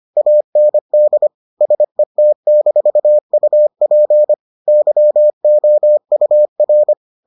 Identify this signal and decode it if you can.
Morse code, heard as a keyed tone, about 25 wpm, 600 Hz. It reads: AND SET-UP YOUR